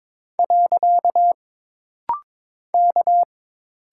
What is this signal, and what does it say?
Morse code, heard as a keyed tone, about 22 wpm, 700 Hz. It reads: . K
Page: This rapid-fire timing is geared to push your ICR (Instant Character Recognition) skills.